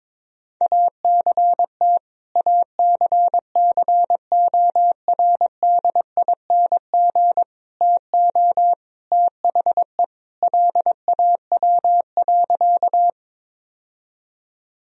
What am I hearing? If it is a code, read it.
Morse code, heard as a keyed tone, about 22 wpm, 700 Hz. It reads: ACT ACCORDING TO THE LAW.
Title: Sentences from Top 1000 Words - Review Part 4 - 22wpm